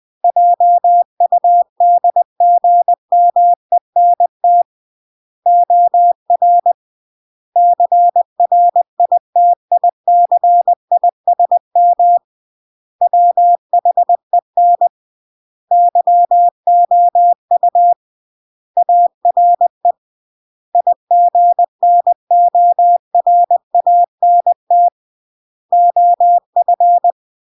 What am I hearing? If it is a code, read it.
Morse code, heard as a keyed tone, about 20 wpm, 700 Hz. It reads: JUDGMENT OR CRITICISM WHEN YOU ARE IGNORANT OF
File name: CW / jodorowsky / 82 commandments 20-20